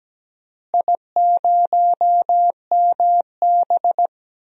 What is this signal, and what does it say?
Morse code, heard as a keyed tone, about 17 wpm, 700 Hz. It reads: I0MB
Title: DXCC-100 Call Signs - 4 Characters - Repeated 3x - 17wpm